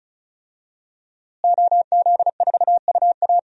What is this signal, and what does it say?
Morse code, heard as a keyed tone, about 36 wpm, 700 Hz. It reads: OZ4UA
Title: DXCC-100 Call Signs - 5 Characters - Repeated 2x - 35wpm